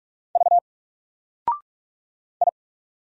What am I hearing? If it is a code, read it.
Morse code, heard as a keyed tone, about 44 wpm, 700 Hz. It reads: V I
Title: Single Character (Letter:Number:Punctuation:Prosign) Alt 45wpm